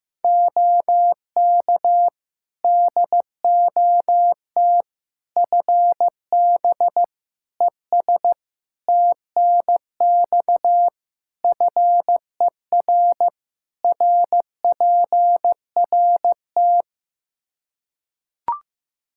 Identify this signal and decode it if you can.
Morse code, heard as a keyed tone, about 16 wpm, 700 Hz. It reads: OK DOT FB ES TNX FER RPRT
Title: Practice QSO - Exchange 2 - Info - 15wpm